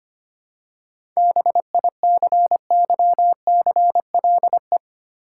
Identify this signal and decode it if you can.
Morse code, heard as a keyed tone, about 25 wpm, 700 Hz. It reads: BICYCLE
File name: practice / 25wpm / Easy Words with Hints - 25wpm